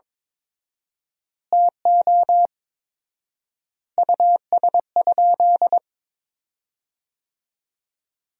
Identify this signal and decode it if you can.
Morse code, heard as a keyed tone, about 22 wpm, 690 Hz. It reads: TO US?
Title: 4x Word-Spacing - Sentences from Top 400 Words - 22wpm